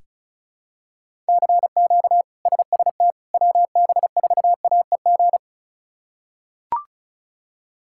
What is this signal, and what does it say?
Morse code, heard as a keyed tone, about 34 wpm, 700 Hz. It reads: CQ SST WB4AEG
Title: K1USN SST QSO Practice - 35wpm